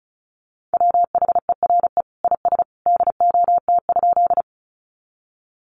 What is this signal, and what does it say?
Morse code, heard as a keyed tone, about 34 wpm, 700 Hz. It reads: WHERE IS DOT?
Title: Sentences from Top 1500 Words 35wpm